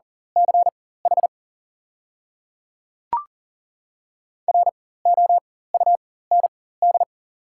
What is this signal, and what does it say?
Morse code, heard as a keyed tone, about 40 wpm, spaced as for 19 wpm, 700 Hz. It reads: CH ROUND
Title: 4x Character Spacing - Top 300 Words - 40wpm